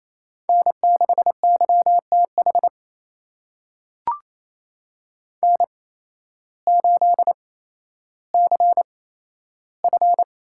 Morse code, heard as a keyed tone, about 28 wpm, spaced as for 13 wpm, 700 Hz. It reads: N6YTH N8CF
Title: USA Call Signs - 5 Characters - 8x Character Spacing - 28wpm